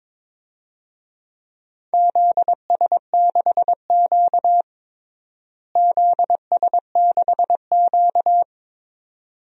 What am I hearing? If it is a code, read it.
Morse code, heard as a keyed tone, about 22 wpm, 700 Hz. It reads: ZS6Q ZS6Q
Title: DXCC-100 Call Signs - 4 Characters - Repeated 2x - 22wpm